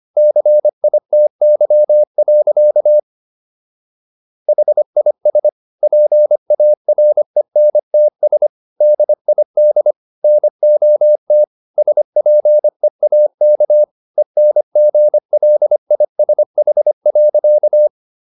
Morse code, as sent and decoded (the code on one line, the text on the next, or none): -.-. .. - -.-- .-.-.- / .... .. ... / .--. .- .-. -. - ... / -.. .. -.. / -. --- - / ... .--. .- -.- / . -. --. .-.. .. ... .... .-.-.-
CITY. HIS PARENTS DID NOT SPEAK ENGLISH.